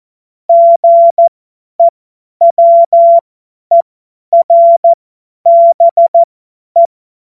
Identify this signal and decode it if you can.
Morse code, heard as a keyed tone, about 14 wpm, spaced as for 10 wpm, 680 Hz. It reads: GEWERBE
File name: Morsekurs / Witze / Witz 27. Januar 2026 50 F70